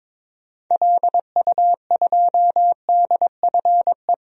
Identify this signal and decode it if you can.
Morse code, heard as a keyed tone, about 22 wpm, 700 Hz. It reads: LU2DFE
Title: DXCC-100 Call Signs - 6 Characters - 22wpm